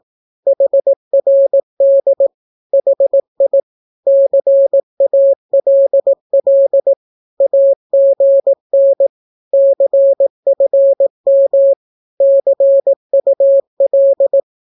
Morse code, heard as a keyed tone, about 18 wpm, 550 Hz. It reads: HRD HI CALL AGN CFM CUL
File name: naukapodstaw / skroty / skroty